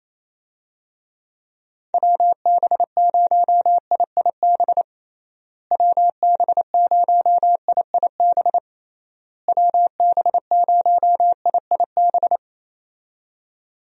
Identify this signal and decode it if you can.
Morse code, heard as a keyed tone, about 28 wpm, 700 Hz. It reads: WB0IIB WB0IIB WB0IIB
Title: USA Call Signs - 6 Characters - Repeated 3x - 28wpm